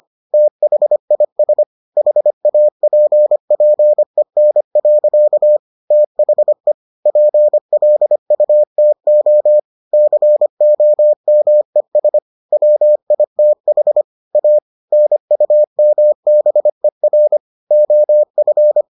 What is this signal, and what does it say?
Morse code, heard as a keyed tone, about 25 wpm, 600 Hz. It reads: THIS HAPPEN. THE PLUTO COMES WITH A NUMBER OF